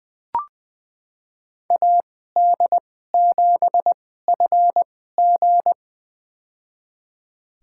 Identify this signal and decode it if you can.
Morse code, heard as a keyed tone, about 20 wpm, spaced as for 14 wpm, 700 Hz. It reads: AD7FG
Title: USA Call Signs - 2x Character Spacing - 20wpm